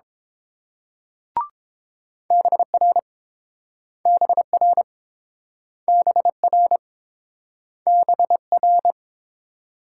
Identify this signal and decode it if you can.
Morse code, heard as a keyed tone, about 31 wpm, 690 Hz. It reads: BR BR BR BR